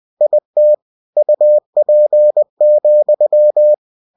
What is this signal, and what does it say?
Morse code, heard as a keyed tone, about 20 wpm, 600 Hz. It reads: IT UP,
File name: cw-books / winnie-the-pooh / winnie-20wpm-page-0027